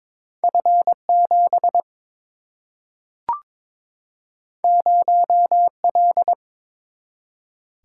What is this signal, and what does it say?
Morse code, heard as a keyed tone, about 22 wpm, 700 Hz. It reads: F 7 0 L
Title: Single-Letter Single-Number 22wpm